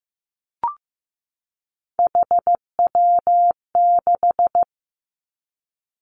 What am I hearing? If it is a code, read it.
Morse code, heard as a keyed tone, about 15 wpm, 700 Hz. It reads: HW6
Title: Three Challenging Letter-Number Combinations 15wpm